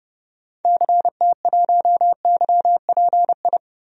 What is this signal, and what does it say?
Morse code, heard as a keyed tone, about 30 wpm, 700 Hz. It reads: CT1YPI
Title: DXCC-100 Call Signs - 6 Characters - Repeated 3x - 30wpm